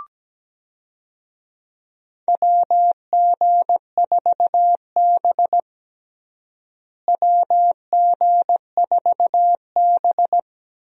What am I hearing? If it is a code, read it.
Morse code, heard as a keyed tone, about 17 wpm, 700 Hz. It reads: WG4B WG4B
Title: USA Call Signs - 4 Characters - Repeated 2x - 17wpm